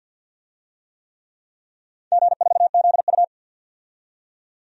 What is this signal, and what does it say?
Morse code, heard as a keyed tone, about 51 wpm, 700 Hz. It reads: G4ZU
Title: DXCC-100 Call Signs - 4 Characters - 50wpm